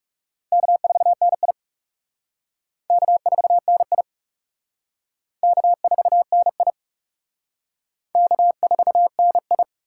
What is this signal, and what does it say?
Morse code, heard as a keyed tone, about 40 wpm, 700 Hz. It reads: K4NI K4NI K4NI K4NI